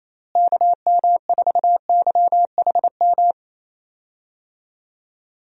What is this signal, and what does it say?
Morse code, heard as a keyed tone, about 28 wpm, 700 Hz. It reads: KM4YHM
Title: USA Call Signs - 6 Characters - Repeated 3x - 28wpm